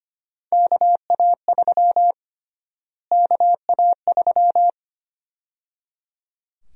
Morse code, heard as a keyed tone, about 25 wpm, 700 Hz. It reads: KA3 KA3
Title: USA Call Sign Prefixes - Repeated 2x - 25wpm